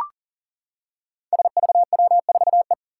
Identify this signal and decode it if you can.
Morse code, heard as a keyed tone, about 38 wpm, 700 Hz. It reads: SVW4E
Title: Five Challenging Letter-Number Combinations 40wpm